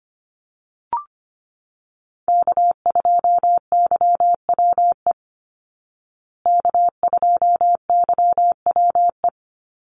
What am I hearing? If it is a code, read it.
Morse code, heard as a keyed tone, about 25 wpm, 700 Hz. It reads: K2YWE K2YWE
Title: Call Signs Repeated 2x - 25wpm